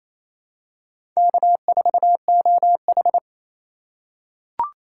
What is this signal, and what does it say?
Morse code, heard as a keyed tone, about 28 wpm, 700 Hz. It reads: K4OH
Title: USA Call Signs - 4 Characters - 3x Character Spacing - 28wpm